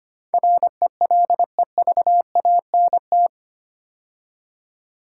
Morse code, heard as a keyed tone, about 25 wpm, 700 Hz. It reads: RELEVANT
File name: 8 Letter Words - First 3 Letters Given - 25wpm